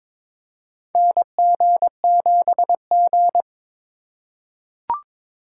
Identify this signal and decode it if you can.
Morse code, heard as a keyed tone, about 22 wpm, 700 Hz. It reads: NG7G
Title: USA Call Signs - 4 Characters - 2x Character Spacing - 22wpm